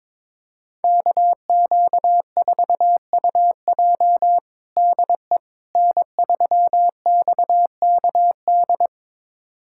Morse code, heard as a keyed tone, about 22 wpm, 700 Hz. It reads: KQ4UJ DE N3XKD
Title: USA Callsign DE Callsign - 5 Characters - 22wpm